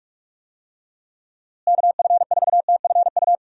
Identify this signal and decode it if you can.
Morse code, heard as a keyed tone, about 44 wpm, 700 Hz. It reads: KF4TFU
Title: USA Call Signs - 6 Characters - Repeated 3x - 45wpm